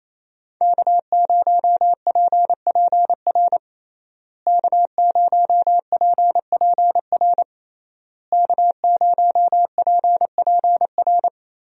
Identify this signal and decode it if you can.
Morse code, heard as a keyed tone, about 27 wpm, 700 Hz. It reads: K0PPR K0PPR K0PPR
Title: USA Call Signs - 5 Characters - Repeated 3x - 28wpm